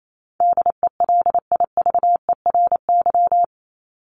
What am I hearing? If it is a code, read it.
Morse code, heard as a keyed tone, about 28 wpm, 700 Hz. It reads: DELIVERY
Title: Top 1251-1500 Words 28wpm